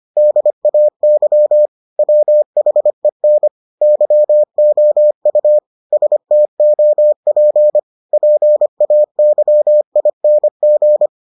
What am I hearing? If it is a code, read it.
Morse code, heard as a keyed tone, about 25 wpm, 600 Hz. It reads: DAY WHEN YOU STOP PAYING